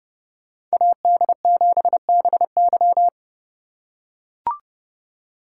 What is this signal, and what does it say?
Morse code, heard as a keyed tone, about 31 wpm, 700 Hz. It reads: AD7BY
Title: USA Call Signs - 3x Character Spacing - 30wpm